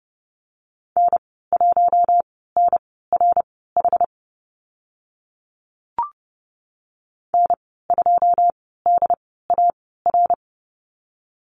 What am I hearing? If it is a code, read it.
Morse code, heard as a keyed tone, about 30 wpm, spaced as for 17 wpm, 690 Hz. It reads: N1NRH N2DAR
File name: Lesson 15g - Number 5 - Call Signs 30wpm